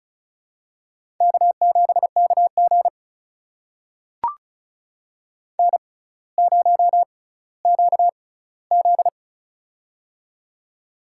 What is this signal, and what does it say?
Morse code, heard as a keyed tone, about 36 wpm, spaced as for 18 wpm, 700 Hz. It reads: K7KG N0QZ
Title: USA Call Signs - 4 Characters - 6x Character Spacing - 35wpm